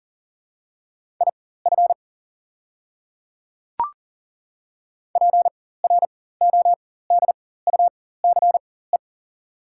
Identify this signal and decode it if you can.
Morse code, heard as a keyed tone, about 39 wpm, spaced as for 18 wpm, 690 Hz. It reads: IF PRODUCE
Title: 4x Character Spacing - Top 500 Words - 40wpm